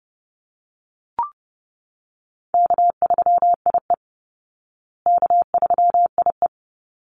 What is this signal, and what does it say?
Morse code, heard as a keyed tone, about 30 wpm, 700 Hz. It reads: K3IE K3IE
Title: Easy Call Signs Repeated 2x - 30wpm